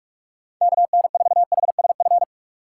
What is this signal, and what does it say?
Morse code, heard as a keyed tone, about 44 wpm, 700 Hz. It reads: KN4HSF